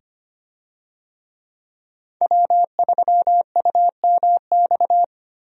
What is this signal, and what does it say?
Morse code, heard as a keyed tone, about 25 wpm, 700 Hz. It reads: W3UMX
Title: USA Call Signs - 5 Characters - 25wpm